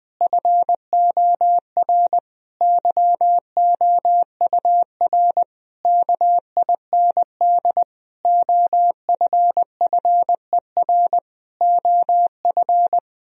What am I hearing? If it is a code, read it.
Morse code, heard as a keyed tone, about 20 wpm, 700 Hz. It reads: FOR YOUR KIND OFFER OF
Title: Sentences from Top 1000 Words - Review Part 4 - 20wpm